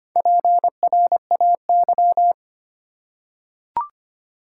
Morse code, heard as a keyed tone, about 25 wpm, 700 Hz. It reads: PRAY
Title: Four Letter Words - 25wpm